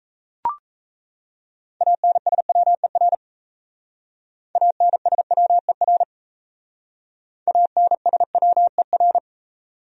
Morse code, heard as a keyed tone, about 35 wpm, 700 Hz. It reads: ANSWER ANSWER ANSWER